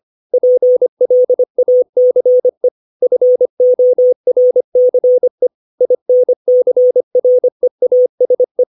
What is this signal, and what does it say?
Morse code, heard as a keyed tone, about 25 wpm, 500 Hz. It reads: PLACE FORCE INCREASE
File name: 632 words / 632 Words 25WPM